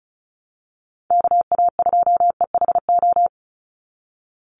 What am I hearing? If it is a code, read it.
Morse code, heard as a keyed tone, about 34 wpm, 700 Hz. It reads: KA2EHO
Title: USA Call Signs 35wpm